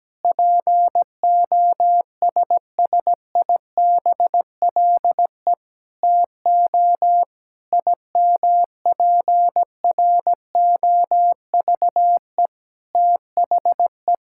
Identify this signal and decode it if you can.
Morse code, heard as a keyed tone, about 17 wpm, 700 Hz. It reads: POSSIBLE TO IMPROVE THE
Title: Sentences from Top 1000 Words - Review Part 5 - 17wpm